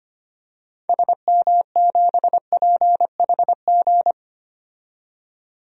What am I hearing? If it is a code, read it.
Morse code, heard as a keyed tone, about 25 wpm, 700 Hz. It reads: SM7PHG
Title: DXCC-100 Call Signs - 6 Characters - 25wpm